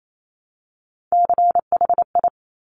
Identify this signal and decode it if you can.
Morse code, heard as a keyed tone, about 28 wpm, 700 Hz. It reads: CHI
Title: Top Three Letter Combinations in Words 28wpm